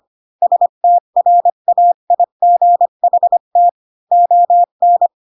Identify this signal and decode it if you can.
Morse code, heard as a keyed tone, about 25 wpm, 700 Hz. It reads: STRAIGHT ON
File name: Pinocchio / Pinocchio ch4 20 25 ik1hge